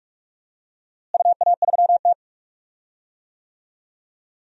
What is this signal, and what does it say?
Morse code, heard as a keyed tone, about 43 wpm, 700 Hz. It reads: UA3T
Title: DXCC-100 Call Signs - 4 Characters - Repeated 3x - 45wpm